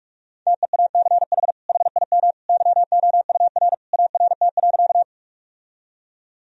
Morse code, heard as a keyed tone, about 48 wpm, 700 Hz. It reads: TEACH HIM YOUR ART.
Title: Sentences from Top 1000 Words - Review Part 3 - 45wpm